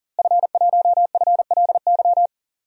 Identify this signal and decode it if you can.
Morse code, heard as a keyed tone, about 40 wpm, 700 Hz. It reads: F1FLY
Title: DXCC-100 Call Signs - 5 Characters - Repeated 3x - 40wpm